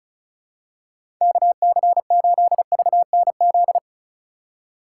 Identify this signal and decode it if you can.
Morse code, heard as a keyed tone, about 34 wpm, 700 Hz. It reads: KC8VNZ
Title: USA Call Signs - 6 Characters - 3x Character Spacing - 35wpm